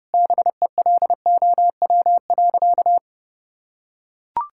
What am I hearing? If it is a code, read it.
Morse code, heard as a keyed tone, about 30 wpm, 700 Hz. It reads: BELOW.
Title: Sentences from Top 1000 Words - Review Part 5 - 30wpm